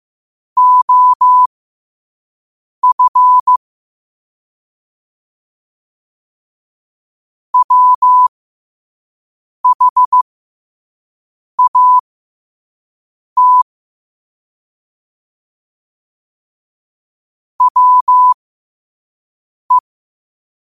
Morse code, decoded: OF WHAT WE

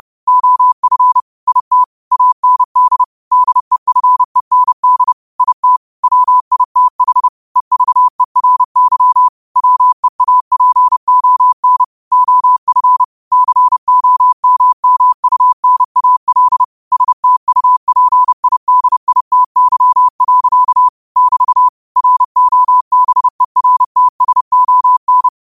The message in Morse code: OF IT AND DEFEND IT WITH EVERY WEAPON OF COMMUNAL STUPIDITY. <BT> ROBERTSON

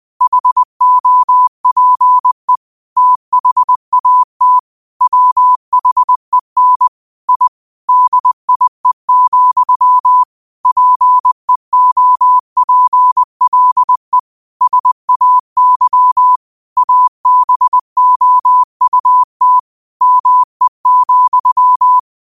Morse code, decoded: HOPE THAT WHEN I DIE, PEOPLE SAY ABOUT ME,

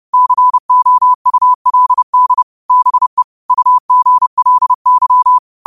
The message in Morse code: COULD BE UGLY

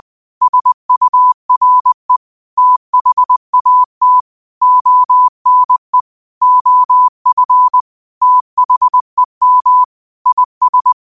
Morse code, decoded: SURE THAT ONE OF THEM IS

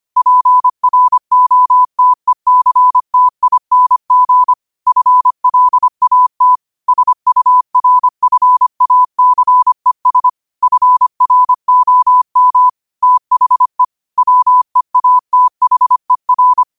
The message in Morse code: PROTECTING FLAT SURFACES FROM THE WEATHER

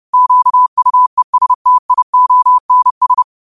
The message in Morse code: QUESTIONS

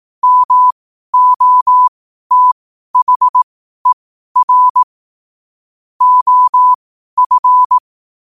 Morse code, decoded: MOTHER OF

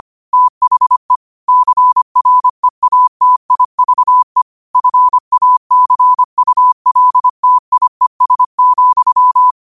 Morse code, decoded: THE CREATIVE FACULTIES,

